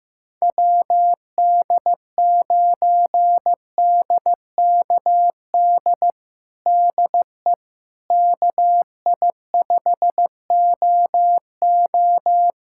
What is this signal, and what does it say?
Morse code, heard as a keyed tone, about 16 wpm, 700 Hz. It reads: WD9DKD DE KI5OO
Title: USA Callsign DE Callsign 15wpm